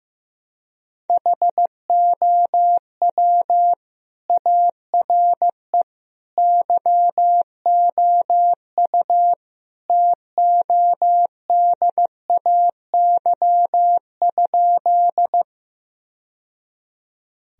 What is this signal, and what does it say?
Morse code, heard as a keyed tone, about 15 wpm, 700 Hz. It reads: HOW ARE YOU TODAY?
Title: Sentences from Top 1000 Words - Review Part 4 - 15wpm